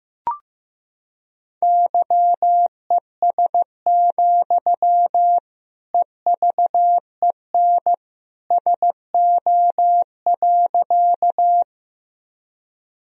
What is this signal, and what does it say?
Morse code, heard as a keyed tone, about 15 wpm, 700 Hz. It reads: YES, EVEN SO.
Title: Sentences from Top 1000 Words - Review Part 1 - 15wpm